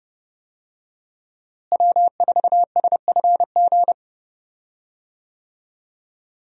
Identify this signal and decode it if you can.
Morse code, heard as a keyed tone, about 31 wpm, 700 Hz. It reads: W4SFG
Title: USA Call Signs - 5 Characters - Repeated 2x - 30wpm